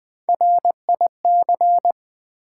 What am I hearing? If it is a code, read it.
Morse code, heard as a keyed tone, about 20 wpm, 700 Hz. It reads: RIC
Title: Build A Word - 1 Letter At A Time - 4-Letter Words - 20wpm